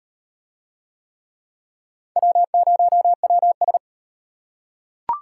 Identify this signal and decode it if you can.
Morse code, heard as a keyed tone, about 33 wpm, 700 Hz. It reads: W0WS